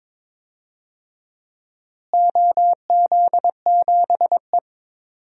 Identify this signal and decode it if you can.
Morse code, heard as a keyed tone, about 21 wpm, 700 Hz. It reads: OZ7E
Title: DXCC-100 Call Signs - 4 Characters - Repeated 3x - 22wpm